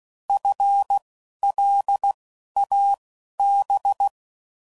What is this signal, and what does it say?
Morse code, heard as a keyed tone, about 16 wpm, spaced as for 11 wpm, 800 Hz. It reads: FLAB
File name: Prefixes / By First Letter / Fs